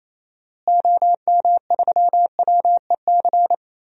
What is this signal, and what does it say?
Morse code, heard as a keyed tone, about 28 wpm, 700 Hz. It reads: OM3WEC
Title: DXCC-100 Call Signs - 6 Characters - Repeated 2x - 28wpm